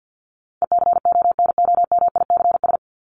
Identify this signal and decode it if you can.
Morse code, heard as a keyed tone, about 50 wpm, 700 Hz. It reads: ECONOMICS
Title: Top 501-750 Words 50wpm